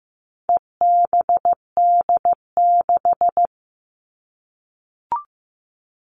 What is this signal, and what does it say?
Morse code, heard as a keyed tone, about 15 wpm, 700 Hz. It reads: EBD6
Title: All Cut Numbers - 4-Digit 15wpm